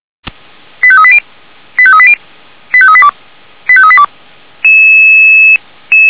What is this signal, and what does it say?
Five-tone selective call (ZVEI), heard as quick runs of five tones, tones of about 70 ms. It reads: 85380 85380 85382 85382
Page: /464/SDR/ZVEI/85859/